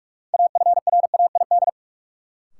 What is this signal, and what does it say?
Morse code, heard as a keyed tone, about 46 wpm, 700 Hz. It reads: AFRAID
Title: Sentences from Top 1500 Words - New Word Drill - 45wpm